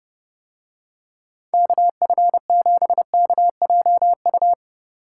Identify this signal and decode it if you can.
Morse code, heard as a keyed tone, about 29 wpm, 700 Hz. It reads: KF7KJU